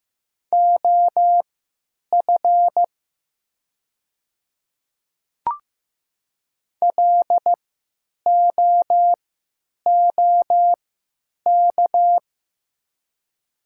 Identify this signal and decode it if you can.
Morse code, heard as a keyed tone, about 15 wpm, spaced as for 9 wpm, 700 Hz. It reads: OF LOOK
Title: 3x Character Spacing - Top 100 Words - 15wpm